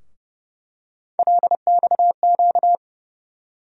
Morse code, heard as a keyed tone, about 30 wpm, 700 Hz. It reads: LXQ
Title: USA Call Sign Postfixes - 3 Letters 30wpm